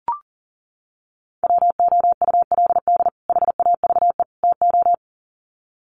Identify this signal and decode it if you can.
Morse code, heard as a keyed tone, about 40 wpm, 700 Hz. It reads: WOULD HAVE TO
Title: Sets of 3 Words 40wpm